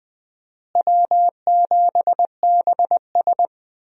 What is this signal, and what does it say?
Morse code, heard as a keyed tone, about 20 wpm, 700 Hz. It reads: W7BS
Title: USA Call Signs - 4 Characters - Repeated 2x - 20wpm